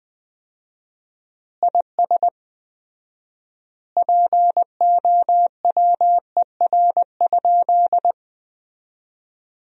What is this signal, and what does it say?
Morse code, heard as a keyed tone, about 21 wpm, 700 Hz. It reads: IS POWER?
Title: 4x Word-Spacing - Sentences from Top 500 Words - 20wpm